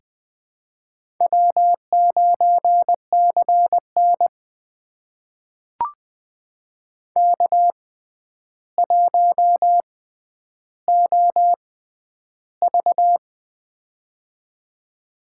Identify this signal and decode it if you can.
Morse code, heard as a keyed tone, about 20 wpm, spaced as for 10 wpm, 700 Hz. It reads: W9CN K1OV
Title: USA Call Signs - 6x Character Spacing - 20wpm